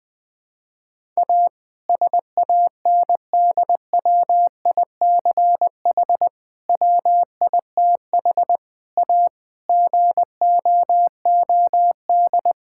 Morse code, heard as a keyed tone, about 20 wpm, 700 Hz. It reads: A SANDWICH WITH A GOOD
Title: All-Age Appropriate Jokes - 20wpm